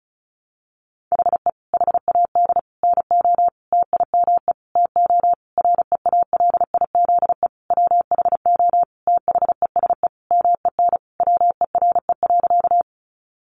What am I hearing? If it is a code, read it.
Morse code, heard as a keyed tone, about 34 wpm, 700 Hz. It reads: HE HAD NO TIME TO REALIZE WHO THESE MEN WERE.
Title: Sentences from Top 1000 Words 35wpm